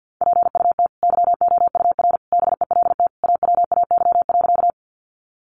Morse code, heard as a keyed tone, about 48 wpm, 700 Hz. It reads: PUT YOUR BELT AWAY.
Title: Sentences from Top 1400 Words 50wpm